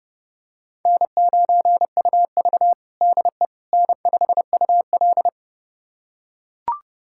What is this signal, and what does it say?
Morse code, heard as a keyed tone, about 30 wpm, 700 Hz. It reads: N9UV DE N5UL
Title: USA Callsign DE Callsign - 4 Characters - 30wpm